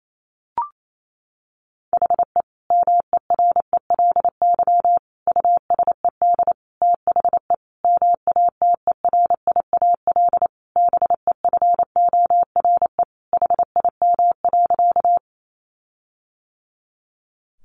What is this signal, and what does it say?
Morse code, heard as a keyed tone, about 28 wpm, 700 Hz. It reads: HE MERELY USED THE MATERIAL BEFORE HIM.
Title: Sentences from Top 1400 Words 28wpm